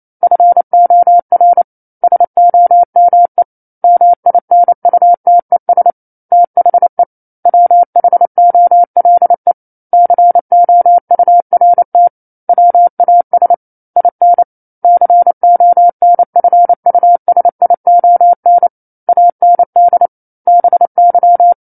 Morse code, heard as a keyed tone, about 28 wpm, 700 Hz. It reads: FOR SOME MINUTES THE WHOLE COURT WAS IN CONFUSION AND BY
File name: 09-Chapter-09 28 WPM